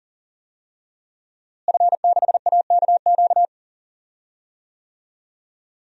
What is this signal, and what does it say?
Morse code, heard as a keyed tone, about 40 wpm, 700 Hz. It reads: F6AKQ
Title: DXCC-100 Call Signs - 5 Characters - Repeated 3x - 40wpm